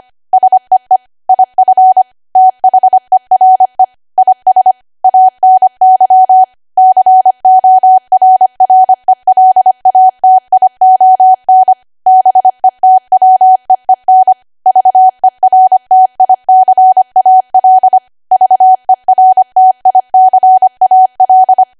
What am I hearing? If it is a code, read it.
Morse code, heard as a keyed tone, about 25 wpm, 750 Hz. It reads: SEE IF THERE IS ANY CORRELATION BETWEEN VERTICAL VERTICAL
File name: Archive / 25 WPM / 25WPM